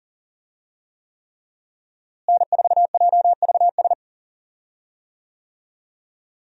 Morse code, heard as a keyed tone, about 37 wpm, 700 Hz. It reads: N4JVS
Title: USA Call Signs - 5 Characters - Repeated 3x - 40wpm